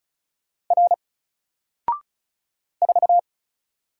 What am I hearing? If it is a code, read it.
Morse code, heard as a keyed tone, about 36 wpm, 700 Hz. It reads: R 4
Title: Single Letter-Number 35wpm